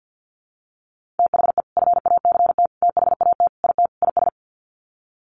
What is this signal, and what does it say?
Morse code, heard as a keyed tone, about 50 wpm, 700 Hz. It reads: THE FACT THAT IT IS
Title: Sets of 5 Words Encore2 50wpm